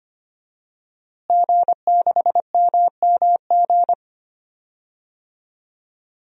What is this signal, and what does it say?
Morse code, heard as a keyed tone, about 25 wpm, 700 Hz. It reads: G6MMG
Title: DXCC-100 Call Signs - 5 Characters - Repeated 3x - 25wpm